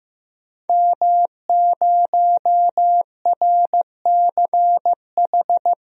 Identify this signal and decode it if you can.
Morse code, heard as a keyed tone, about 15 wpm, 700 Hz. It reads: M0RCH
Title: DXCC-100 Call Signs - 5 Characters - 15wpm